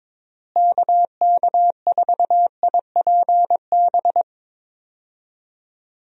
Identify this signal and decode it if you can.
Morse code, heard as a keyed tone, about 21 wpm, 700 Hz. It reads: KK4IPB
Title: USA Call Signs - 6 Characters - Repeated 2x - 22wpm